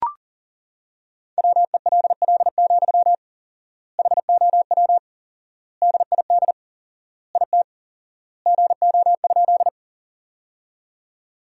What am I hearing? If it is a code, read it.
Morse code, heard as a keyed tone, about 40 wpm, 700 Hz. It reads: WELL, HOW DID IT GO?
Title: 4x Word-Spacing - Sentences from Top 200 Words - 40wpm